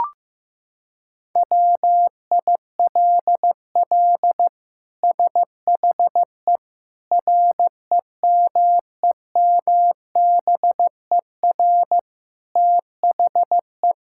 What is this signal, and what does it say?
Morse code, heard as a keyed tone, about 15 wpm, 700 Hz. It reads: WILL SHE REMEMBER THE
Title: Sentences from Top 1000 Words - Review Part 6 - 15wpm